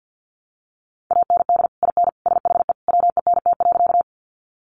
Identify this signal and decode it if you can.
Morse code, heard as a keyed tone, about 52 wpm, 700 Hz. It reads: AND IN SHE WENT.
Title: Sentences from Top 200 Words 50wpm